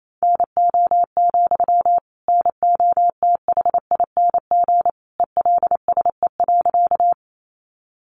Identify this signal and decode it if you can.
Morse code, heard as a keyed tone, about 28 wpm, 700 Hz. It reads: NO, NOTHING ELSE.
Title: Sentences from Top 800 Words 28wpm